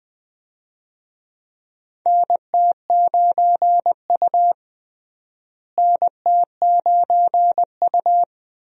Morse code, heard as a keyed tone, about 20 wpm, 700 Hz. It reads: NT9U NT9U
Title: USA Call Signs - 4 Characters - Repeated 3x - 20wpm